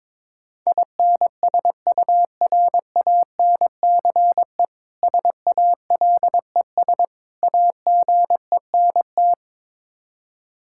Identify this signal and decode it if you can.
Morse code, heard as a keyed tone, about 22 wpm, 700 Hz. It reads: INSURANCE SALES AGENT
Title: K1USN SST QSO Practice 22wpm